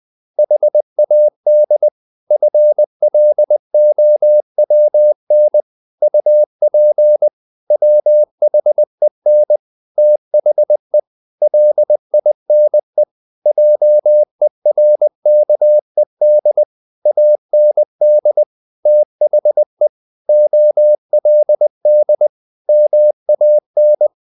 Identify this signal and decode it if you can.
Morse code, heard as a keyed tone, about 20 wpm, 600 Hz. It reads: HAD FLOWN UP WHEN THE LINE JERKED AND THE OLD MAN
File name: the old man and the sea / the old man and the sea part 1 20wpm